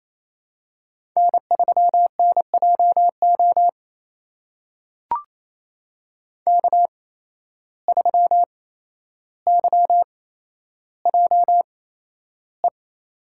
Morse code, heard as a keyed tone, about 27 wpm, spaced as for 12 wpm, 700 Hz. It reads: N3NJO K3YJE
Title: USA Call Signs - 5 Characters - 8x Character Spacing - 28wpm